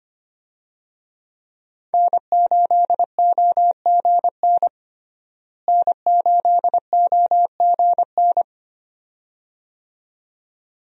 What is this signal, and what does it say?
Morse code, heard as a keyed tone, about 25 wpm, 700 Hz. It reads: N8OGN N8OGN
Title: USA Call Signs - 5 Characters - Repeated 2x - 25wpm